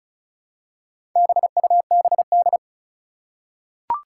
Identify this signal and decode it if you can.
Morse code, heard as a keyed tone, about 36 wpm, 700 Hz. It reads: BUBD
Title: Four Challenging Letter-Number Combinations 35wpm